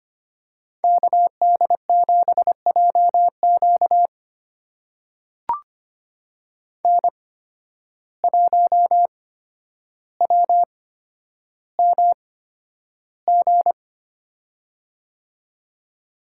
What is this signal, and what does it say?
Morse code, heard as a keyed tone, about 25 wpm, spaced as for 11 wpm, 700 Hz. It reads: KD7JQ N1WMG
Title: USA Call Signs - 5 Characters - 8x Character Spacing - 25wpm